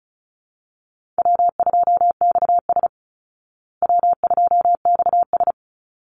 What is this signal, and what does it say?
Morse code, heard as a keyed tone, about 34 wpm, 700 Hz. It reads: W2XS W2XS
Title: Easy Call Signs Repeated 2x - 35wpm